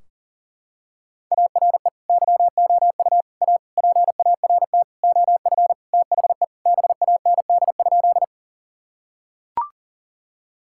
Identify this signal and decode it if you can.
Morse code, heard as a keyed tone, about 40 wpm, 700 Hz. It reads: ARE YOU A PART OF THE BAND?
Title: Sentences from Top 1600 Words 40wpm